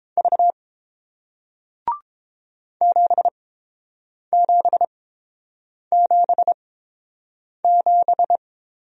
Morse code, heard as a keyed tone, about 29 wpm, spaced as for 7 wpm, 700 Hz. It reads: V 7777